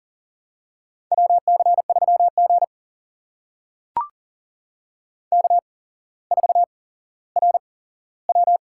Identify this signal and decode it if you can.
Morse code, heard as a keyed tone, about 42 wpm, spaced as for 17 wpm, 700 Hz. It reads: WC3G K4RW
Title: USA Call Signs - 4 Characters - 8x Character Spacing - 40wpm